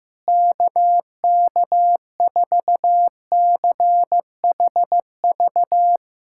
Morse code, heard as a keyed tone, about 15 wpm, 700 Hz. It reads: KK4CHV
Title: USA Callsign DE Callsign 15wpm